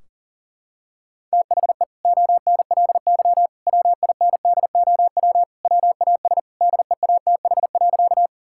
Morse code, heard as a keyed tone, about 40 wpm, 700 Hz. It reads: THE ONLY WINDOW WAS DEATH.
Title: Sentences from Top 1600 Words 40wpm